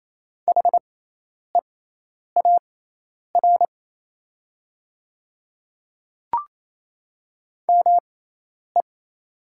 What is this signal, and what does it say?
Morse code, heard as a keyed tone, about 28 wpm, spaced as for 10 wpm, 700 Hz. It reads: HEAR ME
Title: 6x Character Spacing - Top 500 Words - 28wpm